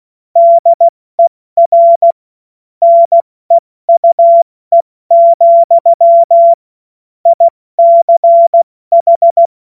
Morse code, heard as a keyed tone, about 16 wpm, spaced as for 14 wpm, 680 Hz. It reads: DER NEUE, ICH
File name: Morsekurs / Witze / Witz 01. März 2023 70 F80